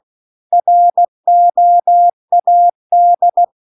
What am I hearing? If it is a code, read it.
Morse code, heard as a keyed tone, about 16 wpm, 700 Hz. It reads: ROAD